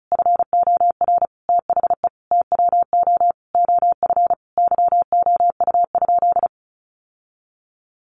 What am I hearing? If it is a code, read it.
Morse code, heard as a keyed tone, about 34 wpm, 700 Hz. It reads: FOR THE TWO OF YOU?
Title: Sentences from Top 1400 Words 35wpm